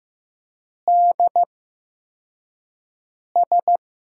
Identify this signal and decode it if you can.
Morse code, heard as a keyed tone, about 15 wpm, spaced as for 4 wpm, 700 Hz. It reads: DS
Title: USA Call Signs - 8x Character Spacing - 15wpm